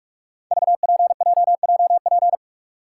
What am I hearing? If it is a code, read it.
Morse code, heard as a keyed tone, about 44 wpm, 700 Hz. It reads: VPJJP
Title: Code Groups - 5 Letters - 45wpm